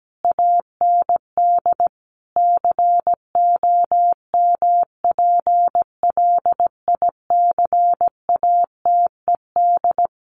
Morse code, heard as a keyed tone, about 17 wpm, 700 Hz. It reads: AND COMPLICATED
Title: Sentences from Top 1200 Words 17wpm